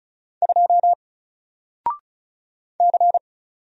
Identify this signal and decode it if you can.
Morse code, heard as a keyed tone, about 35 wpm, 700 Hz. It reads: 2 C
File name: Single-Letter-Number-35wpm